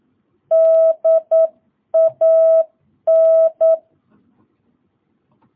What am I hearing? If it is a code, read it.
Morse code, heard as a keyed tone, about 9 wpm, 650 Hz. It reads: DAN